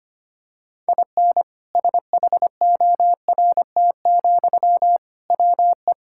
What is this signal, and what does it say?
Morse code, heard as a keyed tone, about 25 wpm, 700 Hz. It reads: IN SHORT, WE
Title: Sentences from Top 1600 Words 25wpm